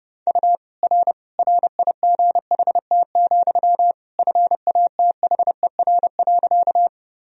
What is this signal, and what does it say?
Morse code, heard as a keyed tone, about 30 wpm, 700 Hz. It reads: U R RIGHT, FATHER.
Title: Sentences from Top 300 Words and Abbreviations 30wpm